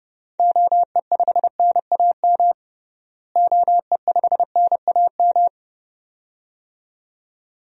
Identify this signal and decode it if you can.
Morse code, heard as a keyed tone, about 30 wpm, 700 Hz. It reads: OE5NAM OE5NAM
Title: DXCC-100 Call Signs - 6 Characters - Repeated 2x - 30wpm